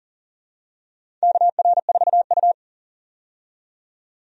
Morse code, heard as a keyed tone, about 40 wpm, 700 Hz. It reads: KR4U
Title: USA Call Signs - 4 Characters - 40wpm